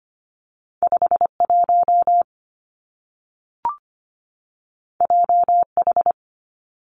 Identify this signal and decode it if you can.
Morse code, heard as a keyed tone, about 25 wpm, 700 Hz. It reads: 51 JH
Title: Two Challenging Letter-Number Combinations 25wpm